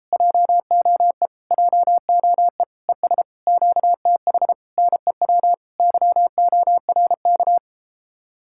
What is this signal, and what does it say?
Morse code, heard as a keyed tone, about 38 wpm, 700 Hz. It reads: JOE JOE ES QTH NEW YORK